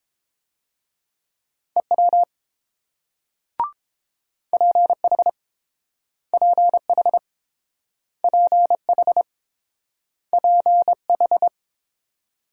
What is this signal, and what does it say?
Morse code, heard as a keyed tone, about 31 wpm, 700 Hz. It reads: EW PH PH PH PH